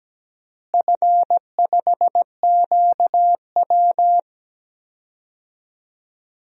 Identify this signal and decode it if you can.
Morse code, heard as a keyed tone, about 17 wpm, 700 Hz. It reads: F5QW
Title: DXCC-100 Call Signs - 4 Characters - Repeated 3x - 17wpm